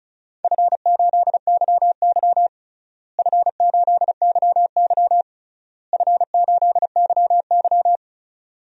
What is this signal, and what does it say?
Morse code, heard as a keyed tone, about 36 wpm, 700 Hz. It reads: F8YY F8YY F8YY
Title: DXCC-100 Call Signs - 4 Characters - Repeated 3x - 35wpm